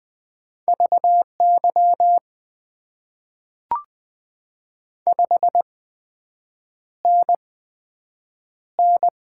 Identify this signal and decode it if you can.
Morse code, heard as a keyed tone, about 20 wpm, spaced as for 7 wpm, 700 Hz. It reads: VY 5NN
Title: Top 90 QSO Elements - 8x Character Spacing - 20wpm